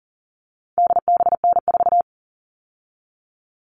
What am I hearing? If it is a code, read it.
Morse code, heard as a keyed tone, about 40 wpm, 700 Hz. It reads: DBN4
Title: All Cut Numbers - 4-Digit 40wpm